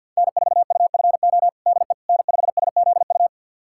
Morse code, heard as a keyed tone, about 51 wpm, 700 Hz. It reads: N4UFO DE N5SZU
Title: USA Callsign DE Callsign - 5 Characters - 50wpm